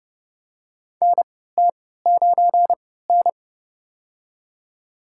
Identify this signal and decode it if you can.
Morse code, heard as a keyed tone, about 30 wpm, spaced as for 17 wpm, 700 Hz. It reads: NT9N
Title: USA Call Signs - 3x Character Spacing - 30wpm